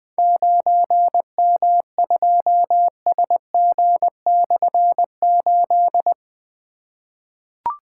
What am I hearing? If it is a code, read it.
Morse code, heard as a keyed tone, about 20 wpm, 700 Hz. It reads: 9M2SG/8
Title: Difficult Call Signs 20wpm